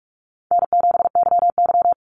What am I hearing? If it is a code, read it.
Morse code, heard as a keyed tone, about 44 wpm, 700 Hz. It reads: N7YY
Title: Easy Call Signs - 45wpm